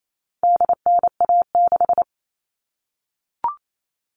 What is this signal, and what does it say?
Morse code, heard as a keyed tone, about 28 wpm, 700 Hz. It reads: DNA6
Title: All Cut Numbers - 4-Digit 28wpm